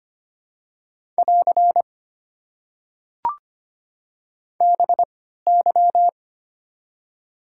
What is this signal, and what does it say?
Morse code, heard as a keyed tone, about 25 wpm, spaced as for 14 wpm, 700 Hz. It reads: <AR> BY
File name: Top 90 QSO Elements - 3x Character Spacing - 25wpm